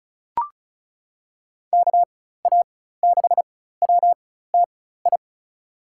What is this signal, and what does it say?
Morse code, heard as a keyed tone, about 36 wpm, spaced as for 16 wpm, 700 Hz. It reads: KA6WTI
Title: USA Call Signs - 6 Characters - 4x Character Spacing - 35wpm